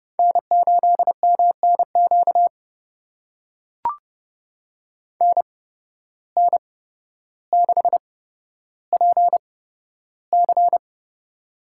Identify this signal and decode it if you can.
Morse code, heard as a keyed tone, about 30 wpm, spaced as for 13 wpm, 700 Hz. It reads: N8MNQ NN6PC
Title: USA Call Signs - 5 Characters - 8x Character Spacing - 30wpm